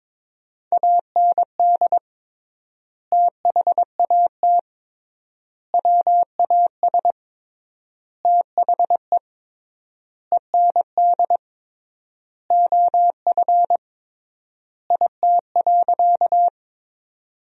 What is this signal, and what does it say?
Morse code, heard as a keyed tone, about 21 wpm, 700 Hz. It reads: AND THAT WAS THE END OF IT.
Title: 3x Word-Spacing - Sentences from Top 200 Words - 22wpm